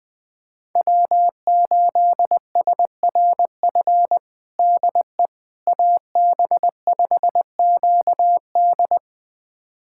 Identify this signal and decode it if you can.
Morse code, heard as a keyed tone, about 20 wpm, 700 Hz. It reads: W8SRF DE AB5QD
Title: USA Callsign DE Callsign - 5 Characters - 20wpm